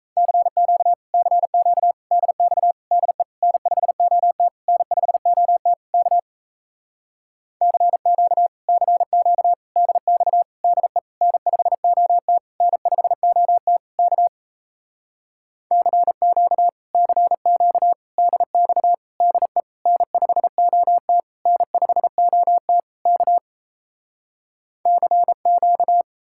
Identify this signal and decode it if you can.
Morse code, heard as a keyed tone, about 37 wpm, 700 Hz. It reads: CQ CQ DX DE N5OT N5OT K CQ CQ DX DE N5OT N5OT K CQ CQ DX DE N5OT N5OT K CQ